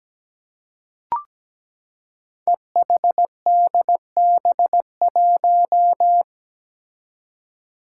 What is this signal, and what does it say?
Morse code, heard as a keyed tone, about 17 wpm, 700 Hz. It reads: EHDB1
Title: Five Challenging Letter-Number Combinations 17wpm